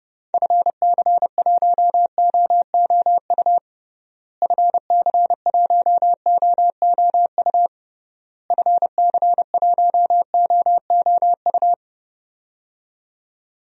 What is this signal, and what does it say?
Morse code, heard as a keyed tone, about 32 wpm, 700 Hz. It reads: FC1OOU FC1OOU FC1OOU
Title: DXCC-100 Call Signs - 6 Characters - Repeated 3x - 30wpm